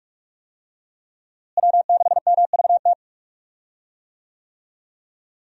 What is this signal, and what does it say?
Morse code, heard as a keyed tone, about 46 wpm, 700 Hz. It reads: W6MVT
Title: USA Call Signs - 5 Characters - Repeated 2x - 45wpm